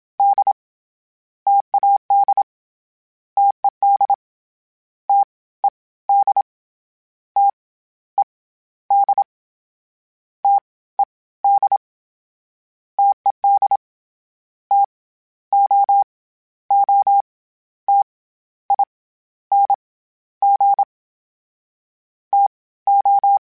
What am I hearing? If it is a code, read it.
Morse code, heard as a keyed tone, about 27 wpm, 800 Hz. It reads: D TAD TED T E D T E D T E D TED T O O T I N G T O
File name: Variable Speed / Variable Speed-mp3s / T AOEI's sp @25